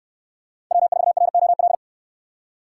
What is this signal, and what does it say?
Morse code, heard as a keyed tone, about 68 wpm, 700 Hz. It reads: W4ROL